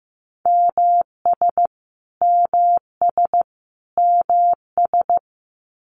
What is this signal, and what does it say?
Morse code, heard as a keyed tone, about 15 wpm, 700 Hz. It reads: MS MS MS
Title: US States Repeated 3x 15wpm